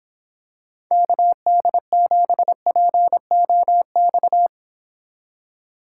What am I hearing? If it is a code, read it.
Morse code, heard as a keyed tone, about 26 wpm, 700 Hz. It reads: KD7POX